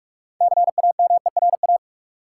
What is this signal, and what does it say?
Morse code, heard as a keyed tone, about 45 wpm, 700 Hz. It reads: CAMERA
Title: Six Letter Words - 45wpm